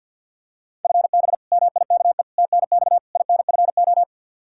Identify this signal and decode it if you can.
Morse code, heard as a keyed tone, about 46 wpm, 700 Hz. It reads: FB MIKE TNX INFO
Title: Practice QSO - Exchange 3 - More Info - 50wpm